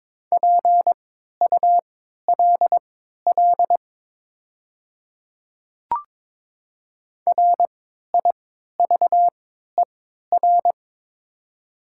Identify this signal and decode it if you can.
Morse code, heard as a keyed tone, about 22 wpm, spaced as for 13 wpm, 700 Hz. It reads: PULL RIVER
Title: 3x Character Spacing - Top 500 Words - 22wpm